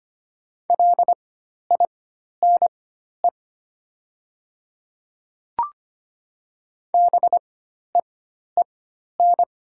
Morse code, heard as a keyed tone, about 25 wpm, spaced as for 12 wpm, 700 Hz. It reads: LINE BEEN
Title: 4x Character Spacing - Top 200 Words - 25wpm